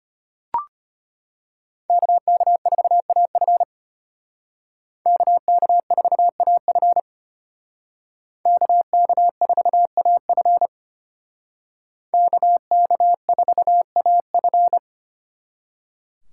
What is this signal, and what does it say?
Morse code, heard as a keyed tone, about 30 wpm, 700 Hz. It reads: KK4AF KK4AF KK4AF KK4AF
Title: Speed Racing - USA 2x2 Call Signs 25wpm